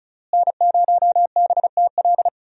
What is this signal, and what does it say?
Morse code, heard as a keyed tone, about 36 wpm, 700 Hz. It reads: N0BTL
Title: USA Call Signs - 5 Characters - Repeated 3x - 35wpm